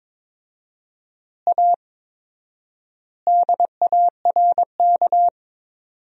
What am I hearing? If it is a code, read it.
Morse code, heard as a keyed tone, about 22 wpm, 700 Hz. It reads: A DARK
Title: 4x Word-Spacing - Sentences from Top 500 Words - 22wpm